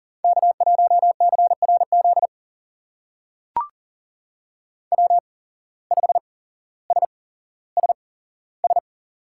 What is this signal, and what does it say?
Morse code, heard as a keyed tone, about 39 wpm, spaced as for 17 wpm, 700 Hz. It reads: K1CRZ W5SSS
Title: USA Call Signs - 5 Characters - 8x Character Spacing - 40wpm